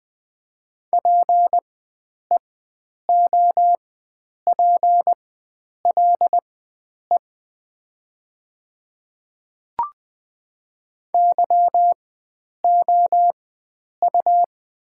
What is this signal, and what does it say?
Morse code, heard as a keyed tone, about 20 wpm, spaced as for 9 wpm, 700 Hz. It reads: PEOPLE YOU
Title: 4x Character Spacing - Top 100 Words - 20wpm